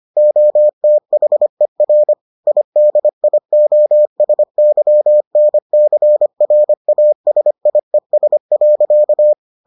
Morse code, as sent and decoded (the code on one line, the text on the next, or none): --- - .... .-. / .. -.. .. --- ... -.-- -. -.-. .-. .- ... .. ... .-.-.-
OTHER IDIOSYNCRASIES.